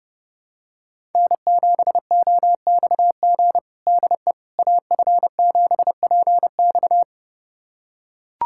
Morse code, heard as a keyed tone, about 30 wpm, 700 Hz. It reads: N7OXG DE AF7PX
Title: USA Callsign DE Callsign - 5 Characters - 30wpm